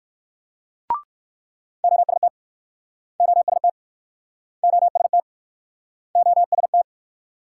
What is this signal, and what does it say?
Morse code, heard as a keyed tone, about 61 wpm, 700 Hz. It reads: OST OST OST OST